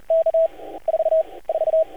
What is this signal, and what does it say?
Morse code, heard as a keyed tone, about 30 wpm, 650 Hz. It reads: K 4 4
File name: archived / CRSnet